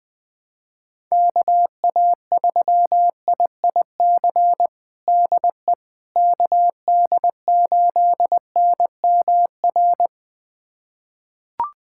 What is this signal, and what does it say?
Morse code, heard as a keyed tone, about 20 wpm, 700 Hz. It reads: KA3IIC DE KD8NMR
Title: USA Callsign DE Callsign - 6 Characters - 20wpm